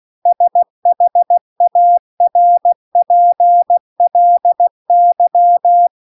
Morse code, heard as a keyed tone, about 16 wpm, 700 Hz. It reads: SHARPLY
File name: CW-The-Big-Twoheaded-River-2-16WPM